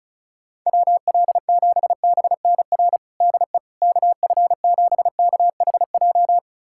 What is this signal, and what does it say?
Morse code, heard as a keyed tone, about 34 wpm, 700 Hz. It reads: WL7BNR DE KF7KHJ
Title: USA Callsign DE Callsign - 6 Characters - 35wpm